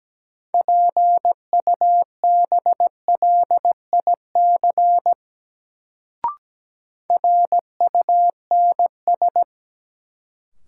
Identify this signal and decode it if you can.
Morse code, heard as a keyed tone, about 17 wpm, 700 Hz. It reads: PUBLIC RUNS
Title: Sentences from Top 600 Words - New Word Drill - 17wpm